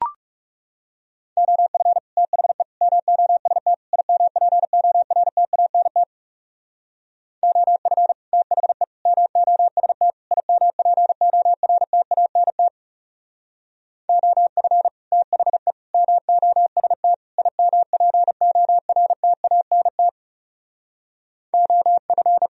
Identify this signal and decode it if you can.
Morse code, heard as a keyed tone, about 35 wpm, 700 Hz. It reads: OF THE MOST IMPORTANT OF THE MOST IMPORTANT OF THE MOST IMPORTANT OF
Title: Speed-Racing - Sets of 4 Words 30wpm